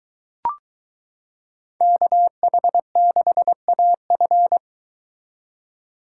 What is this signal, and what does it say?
Morse code, heard as a keyed tone, about 23 wpm, 700 Hz. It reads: KH6AF